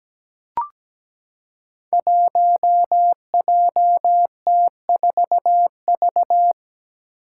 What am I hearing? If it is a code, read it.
Morse code, heard as a keyed tone, about 17 wpm, 700 Hz. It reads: 1JT4V
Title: Five Challenging Letter-Number Combinations 17wpm